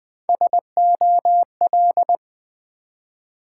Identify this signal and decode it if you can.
Morse code, heard as a keyed tone, about 20 wpm, 700 Hz. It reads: SOL
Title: Build A Word - 1 Letter At A Time - 5-Letter Words - 20wpm